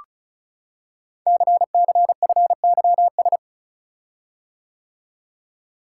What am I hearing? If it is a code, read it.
Morse code, heard as a keyed tone, about 36 wpm, 700 Hz. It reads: CCFYS
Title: Code Groups - 5 Letters - 35wpm